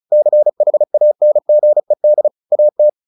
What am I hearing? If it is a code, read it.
Morse code, heard as a keyed tone, about 34 wpm, 600 Hz. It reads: CHANGED AT
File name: the old man and the sea 2 35wpm